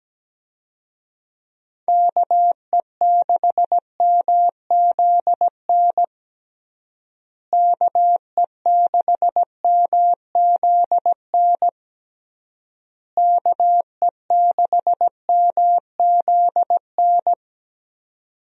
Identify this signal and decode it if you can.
Morse code, heard as a keyed tone, about 17 wpm, 700 Hz. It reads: KE6MZN KE6MZN KE6MZN
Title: USA Call Signs - 6 Characters - Repeated 3x - 17wpm